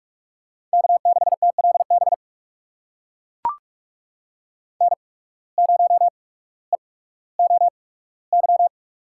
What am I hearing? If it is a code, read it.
Morse code, heard as a keyed tone, about 42 wpm, spaced as for 19 wpm, 700 Hz. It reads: K6TLB N0EOY
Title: USA Call Signs - 5 Characters - 8x Character Spacing - 45wpm